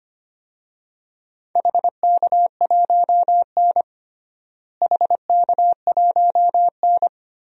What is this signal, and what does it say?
Morse code, heard as a keyed tone, about 25 wpm, 700 Hz. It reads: HK1N HK1N
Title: DXCC-100 Call Signs - 4 Characters - Repeated 3x - 25wpm